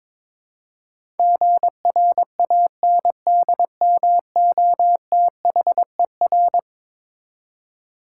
Morse code, heard as a keyed tone, about 22 wpm, 700 Hz. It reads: GRANDMOTHER
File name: practice / 22wpm / Eleven Letter Words - 22wpm